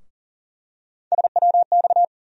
Morse code, heard as a keyed tone, about 40 wpm, 700 Hz. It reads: SWX
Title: USA Call Sign Postfixes - 3 Letters 40wpm